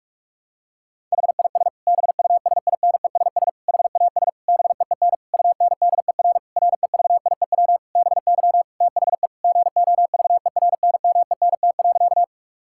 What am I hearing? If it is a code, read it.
Morse code, heard as a keyed tone, about 46 wpm, 700 Hz. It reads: HIS BUSINESS HAS BEEN UNDER REVIEW BY THE GOVERNMENT.
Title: Sentences from Top 1000 Words - Review Part 6 - 45wpm